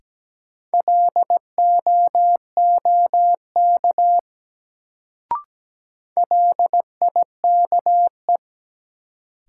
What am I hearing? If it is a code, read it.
Morse code, heard as a keyed tone, about 17 wpm, 700 Hz. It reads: LOOK LIKE
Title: Sets of 2 Words - New Word Drill - 17wpm